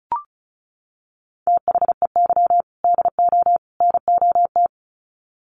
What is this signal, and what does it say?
Morse code, heard as a keyed tone, about 34 wpm, 700 Hz. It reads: THEY DO NOT
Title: Sets of 3 Words 35wpm